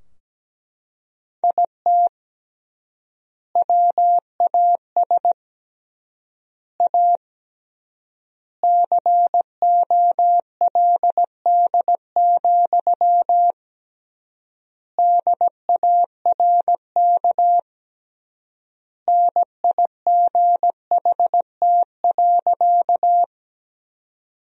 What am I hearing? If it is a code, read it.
Morse code, heard as a keyed tone, about 17 wpm, 700 Hz. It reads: IT WAS A COLD, DARK NIGHT.
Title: 3x Word-Spacing - Sentences from Top 500 Words - 17wpm